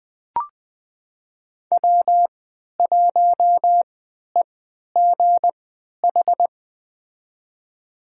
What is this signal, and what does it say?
Morse code, heard as a keyed tone, about 20 wpm, spaced as for 11 wpm, 700 Hz. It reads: W1EGH
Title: USA Call Signs - 5 Characters - 3x Character Spacing - 20wpm